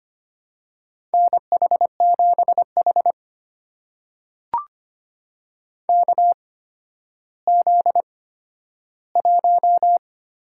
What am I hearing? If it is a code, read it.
Morse code, heard as a keyed tone, about 25 wpm, spaced as for 12 wpm, 700 Hz. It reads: NH7H KZ1
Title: USA Call Signs - 4 Characters - 8x Character Spacing - 25wpm